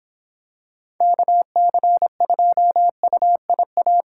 Morse code, heard as a keyed tone, about 25 wpm, 700 Hz. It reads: KC2UIA